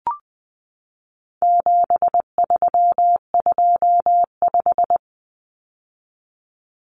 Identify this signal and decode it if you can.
Morse code, heard as a keyed tone, about 20 wpm, 700 Hz. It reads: 7325
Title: Numbers - 4-Digit 20wpm